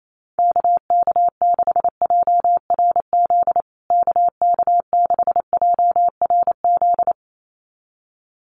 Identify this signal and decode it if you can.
Morse code, heard as a keyed tone, about 28 wpm, 700 Hz. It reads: KK6JRZ KK6JRZ
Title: USA Call Signs Repeated 2x 28wpm